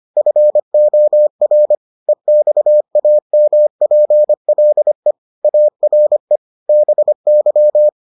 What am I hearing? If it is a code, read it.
Morse code, heard as a keyed tone, about 25 wpm, 600 Hz. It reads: FOR EXAMPLE ARE BY